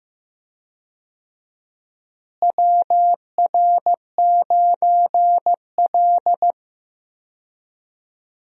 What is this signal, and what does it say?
Morse code, heard as a keyed tone, about 15 wpm, 700 Hz. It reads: WR9L
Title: USA Call Signs - 4 Characters - Repeated 3x - 15wpm